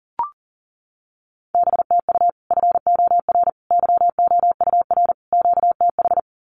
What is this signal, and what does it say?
Morse code, heard as a keyed tone, about 40 wpm, 700 Hz. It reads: BTU FOR YOUR QTH
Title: Common QSO Phrases 40wpm